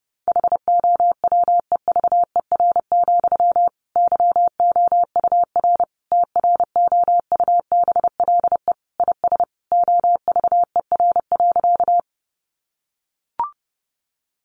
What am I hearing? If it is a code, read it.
Morse code, heard as a keyed tone, about 30 wpm, 700 Hz. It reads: HOWEVER, YOUR TROUBLE IS OVER.
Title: Sentences from Top 1200 Words 30wpm